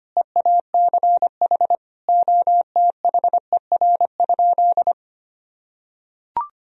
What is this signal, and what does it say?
Morse code, heard as a keyed tone, about 25 wpm, 700 Hz. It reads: EACH OTHER?
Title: Sentences from Top 1000 Words - Review Part 1 - 25wpm